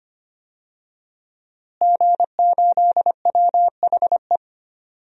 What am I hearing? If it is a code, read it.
Morse code, heard as a keyed tone, about 25 wpm, 700 Hz. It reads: G8WHE
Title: DXCC-100 Call Signs - 5 Characters - Repeated 2x - 25wpm